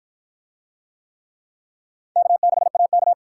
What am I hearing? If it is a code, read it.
Morse code, heard as a keyed tone, about 46 wpm, 700 Hz. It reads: K6AK